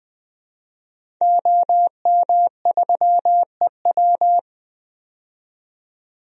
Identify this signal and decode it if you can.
Morse code, heard as a keyed tone, about 20 wpm, 700 Hz. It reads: OM3EW
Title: DXCC-100 Call Signs - 5 Characters - 20wpm